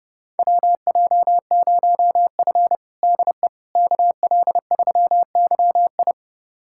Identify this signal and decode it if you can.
Morse code, heard as a keyed tone, about 32 wpm, 700 Hz. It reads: WJ0F DE KL3YI
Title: USA Callsign DE Callsign 30wpm